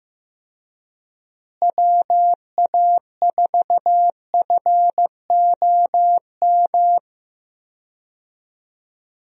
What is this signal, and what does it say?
Morse code, heard as a keyed tone, about 15 wpm, 700 Hz. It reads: WA4FOM
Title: USA Call Signs - 6 Characters - Repeated 3x - 15wpm